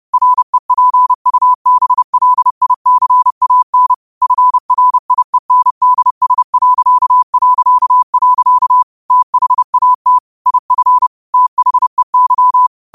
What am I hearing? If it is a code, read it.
Morse code, heard as a keyed tone, about 30 wpm, 1000 Hz. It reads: REPUBLICAN FRIENDS... THAT IF THEY
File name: qotdfiles / 30wpm